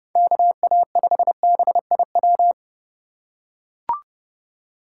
Morse code, heard as a keyed tone, about 31 wpm, 700 Hz. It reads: KA5BIW
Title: USA Call Signs - 6 Characters - 4x Character Spacing - 30wpm